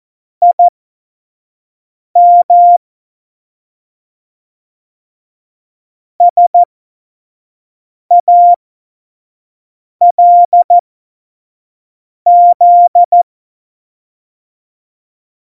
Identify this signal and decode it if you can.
Morse code, heard as a keyed tone, about 14 wpm, spaced as for 5 wpm, 700 Hz. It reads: IM SALZ
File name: Morsekurs / Witze / Witz 02. Juli 2019, 25 F70